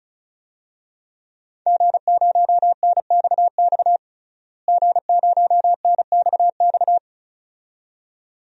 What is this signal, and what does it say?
Morse code, heard as a keyed tone, about 36 wpm, 700 Hz. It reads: G0NXX G0NXX
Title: DXCC-100 Call Signs - 5 Characters - Repeated 2x - 35wpm